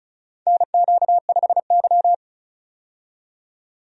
Morse code, heard as a keyed tone, about 36 wpm, 700 Hz. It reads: NQ5Y
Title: USA Call Signs - 4 Characters - 35wpm